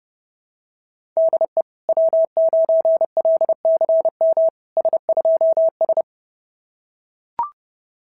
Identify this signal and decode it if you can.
Morse code, heard as a keyed tone, about 31 wpm, 690 Hz. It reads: DE W9LCM S2S
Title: SOTA QSO Practice 30wpm